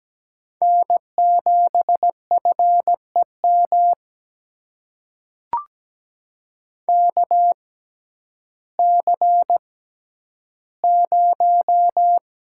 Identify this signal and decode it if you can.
Morse code, heard as a keyed tone, about 17 wpm, spaced as for 10 wpm, 700 Hz. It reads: N7FEM KC0
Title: USA Call Signs - 5 Characters - 6x Character Spacing - 17wpm